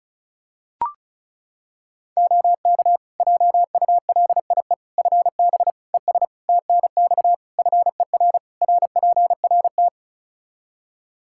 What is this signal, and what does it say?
Morse code, heard as a keyed tone, about 36 wpm, 700 Hz. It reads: OK JULIE FB ES TNX FER RPRT
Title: Practice QSO - Exchange 2 - Info - 35wpm